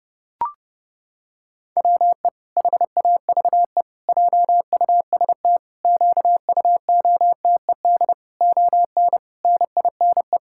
Morse code, heard as a keyed tone, about 32 wpm, 700 Hz. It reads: WE HAVE JUST QUOTED ON NINE
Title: Pangram Challenge v2 - 30wpm